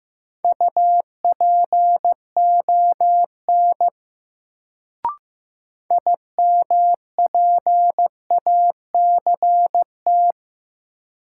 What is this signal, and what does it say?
Morse code, heard as a keyed tone, about 15 wpm, 700 Hz. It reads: UPON IMPACT
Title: Sentences from Top 1000 Words - Review Part 6 - New Word Drill - 15wpm